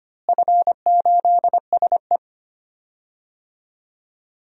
Morse code, heard as a keyed tone, about 25 wpm, 700 Hz. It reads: F8SE
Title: DXCC-100 Call Signs - 4 Characters - Repeated 2x - 25wpm